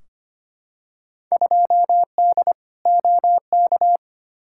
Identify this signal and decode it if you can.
Morse code, heard as a keyed tone, about 25 wpm, 700 Hz. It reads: 2D OK
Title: Field Day Exchanges Encore - 25wpm